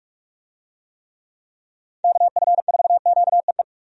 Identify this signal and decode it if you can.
Morse code, heard as a keyed tone, about 42 wpm, 700 Hz. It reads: KF4QEE
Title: USA Call Signs - 6 Characters - 45wpm